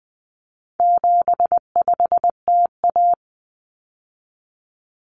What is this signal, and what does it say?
Morse code, heard as a keyed tone, about 20 wpm, 700 Hz. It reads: 75TA
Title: Common Cut Numbers - 4-Digit 20wpm